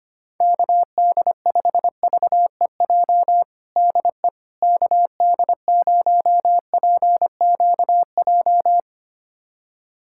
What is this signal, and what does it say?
Morse code, heard as a keyed tone, about 25 wpm, 700 Hz. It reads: KD5VEJ DE KD0PQJ
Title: USA Callsign DE Callsign - 6 Characters - 25wpm